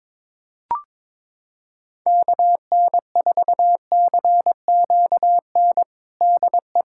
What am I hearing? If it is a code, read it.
Morse code, heard as a keyed tone, about 22 wpm, 700 Hz. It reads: KN4CQN DE
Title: USA Callsign DE Callsign - 6 Characters - 22wpm